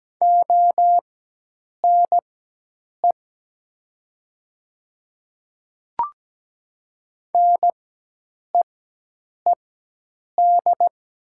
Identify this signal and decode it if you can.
Morse code, heard as a keyed tone, about 17 wpm, spaced as for 8 wpm, 700 Hz. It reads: ONE NEED
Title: 4x Character Spacing - Top 200 Words - 17wpm